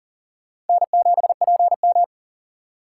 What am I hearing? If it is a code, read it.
Morse code, heard as a keyed tone, about 50 wpm, 700 Hz. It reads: N7PM
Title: Speed Racing - USA 1x2 Call Signs 40wpm